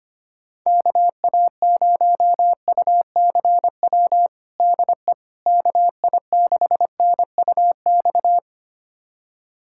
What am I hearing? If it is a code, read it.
Morse code, heard as a keyed tone, about 25 wpm, 700 Hz. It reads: KA0UCW DE KI6NUX
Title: USA Callsign DE Callsign - 6 Characters - 25wpm